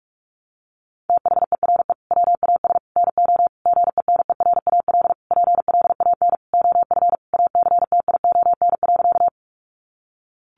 Morse code, heard as a keyed tone, about 44 wpm, 700 Hz. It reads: THERE WAS NO GENERAL PLAN OF ACTION.
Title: Sentences from Top 700 Words 45wpm